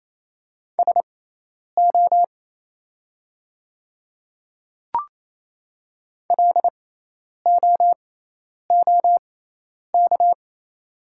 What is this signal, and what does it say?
Morse code, heard as a keyed tone, about 28 wpm, spaced as for 10 wpm, 700 Hz. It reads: SO LOOK
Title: 6x Character Spacing - Top 100 Words - 28wpm